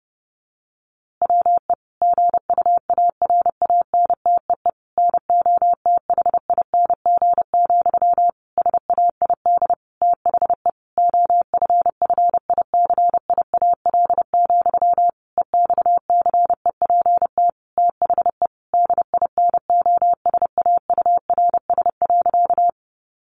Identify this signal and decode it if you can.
Morse code, heard as a keyed tone, about 30 wpm, 700 Hz. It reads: WE GUARANTEE NOTHING, SAID THE OFFICIAL, EXCEPT THE DINOSAURS.
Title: A Sound Of Thunder 30wpm